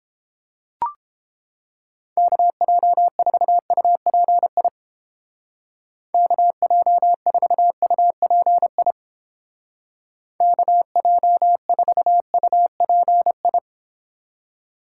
Morse code, heard as a keyed tone, about 30 wpm, 700 Hz. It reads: KJ4UPI KJ4UPI KJ4UPI